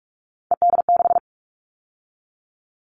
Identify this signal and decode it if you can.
Morse code, heard as a keyed tone, about 44 wpm, 700 Hz. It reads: ED6
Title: Callsign Prefixes for Top 100 DXCC - 45wpm